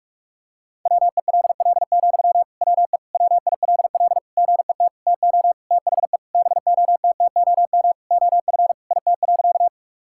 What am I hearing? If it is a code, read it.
Morse code, heard as a keyed tone, about 45 wpm, 700 Hz. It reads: WELL, WE WILL GET TO THE BOTTOM OF IT.
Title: Sentences from Top 1000 Words - Review Part 6 - 45wpm